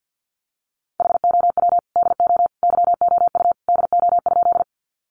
Morse code, heard as a keyed tone, about 50 wpm, 700 Hz. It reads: HOW DO YOU DO?
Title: Sentences from Top 100 Words 50wpm